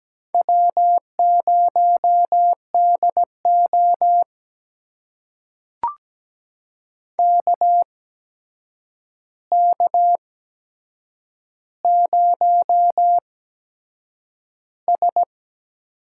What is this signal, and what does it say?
Morse code, heard as a keyed tone, about 17 wpm, spaced as for 7 wpm, 700 Hz. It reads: W0DO KK0S
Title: USA Call Signs - 4 Characters - 8x Character Spacing - 17wpm